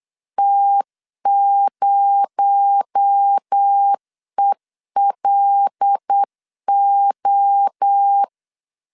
Morse code, heard as a keyed tone, about 8 wpm, 800 Hz. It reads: T0ELO
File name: LEZIONI / 6 PKX0 / 6 PKX0 8wpm 3341